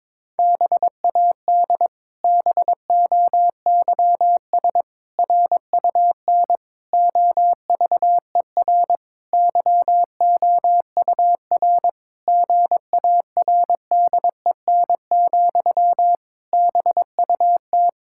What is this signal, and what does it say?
Morse code, heard as a keyed tone, about 22 wpm, 700 Hz. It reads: BAD BOYS RUN OVER YOUR GARDEN, BUT
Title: Pangram Challenge v2 - 22wpm